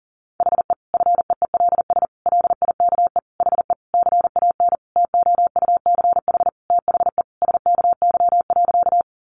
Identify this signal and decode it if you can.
Morse code, heard as a keyed tone, about 40 wpm, 700 Hz. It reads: HE FEELS LIKE HE CAN TOUCH THE SKY.
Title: Additional Easy Phrases 40wpm